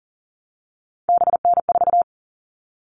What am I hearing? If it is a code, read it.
Morse code, heard as a keyed tone, about 40 wpm, 700 Hz. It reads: BN4
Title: All Cut Numbers - 3-Digit 40wpm